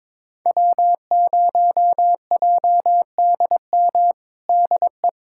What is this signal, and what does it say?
Morse code, heard as a keyed tone, about 22 wpm, 700 Hz. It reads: W0JDM DE
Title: USA Callsign DE Callsign 22wpm